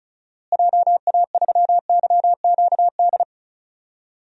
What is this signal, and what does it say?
Morse code, heard as a keyed tone, about 36 wpm, 700 Hz. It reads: JA3YQD
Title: DXCC-100 Call Signs - 6 Characters - Repeated 3x - 35wpm